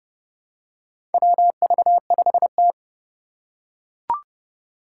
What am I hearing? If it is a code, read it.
Morse code, heard as a keyed tone, about 28 wpm, 690 Hz. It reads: WV5T
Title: Four Challenging Letter-Number Combinations 30wpm